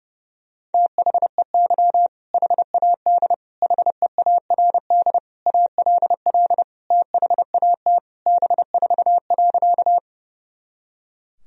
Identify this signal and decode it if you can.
Morse code, heard as a keyed tone, about 30 wpm, 700 Hz. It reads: THEY HAD HEARD ALL THAT B4.
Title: Sentences from Top 400 Words and Abbreviations 30wpm